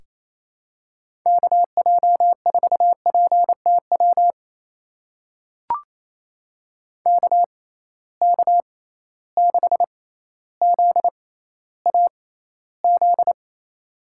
Morse code, heard as a keyed tone, about 28 wpm, spaced as for 14 wpm, 700 Hz. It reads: KJ4PTW KK6ZAZ
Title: USA Call Signs - 6 Characters - 6x Character Spacing - 28wpm